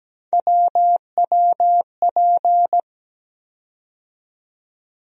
Morse code, heard as a keyed tone, about 17 wpm, 700 Hz. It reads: WWP
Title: USA Call Sign Postfixes - 3 Letters 17wpm